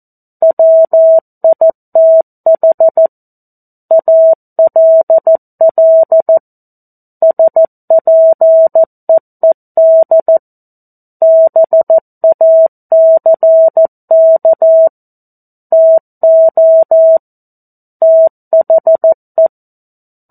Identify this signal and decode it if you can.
Morse code, heard as a keyed tone, about 14 wpm, 650 Hz. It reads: WITH ALL SPEED BACK TO THE
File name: 02-Chapter-02 14 WPM